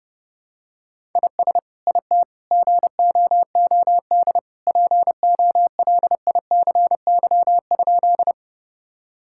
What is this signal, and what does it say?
Morse code, heard as a keyed tone, about 30 wpm, 700 Hz. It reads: IS IT GOOD POLICY?
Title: Sentences from Top 1000 Words - Review Part 5 - 30wpm